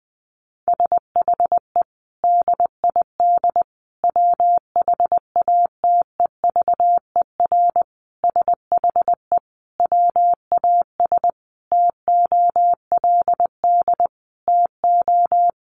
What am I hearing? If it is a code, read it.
Morse code, heard as a keyed tone, about 20 wpm, 700 Hz. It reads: SHE DID WHATEVER SHE WAS TOLD TO
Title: Sentences from Top 1100 Words 20wpm